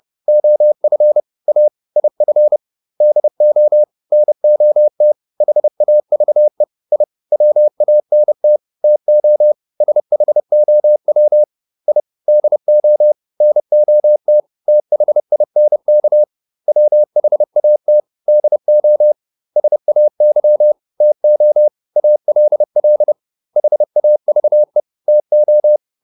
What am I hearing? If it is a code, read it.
Morse code, heard as a keyed tone, about 30 wpm, 600 Hz. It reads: OF A IF DO NOT HAVE I WANT TO SHOW I DO NOT THINK WHAT DO SAY TO ALL HAVE TO